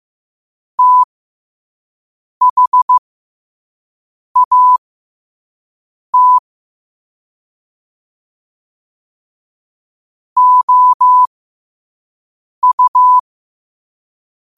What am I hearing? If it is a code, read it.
Morse code, heard as a keyed tone, about 15 wpm, spaced as for 5 wpm, 1000 Hz. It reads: THAT OU